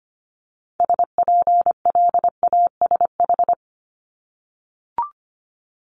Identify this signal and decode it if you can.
Morse code, heard as a keyed tone, about 25 wpm, 700 Hz. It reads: SPLASH
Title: Multi-letter Phonemes 25wpm